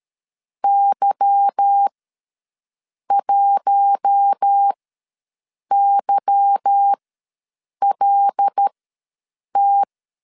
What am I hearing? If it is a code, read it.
Morse code, heard as a keyed tone, about 13 wpm, spaced as for 6 wpm, 800 Hz. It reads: Y1YLT
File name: LEZIONI / 8 YJ45 / 8 Y ws 12wpm 3409